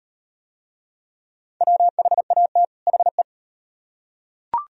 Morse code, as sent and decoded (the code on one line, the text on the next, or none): .-- .... .- - / .... .
WHAT HE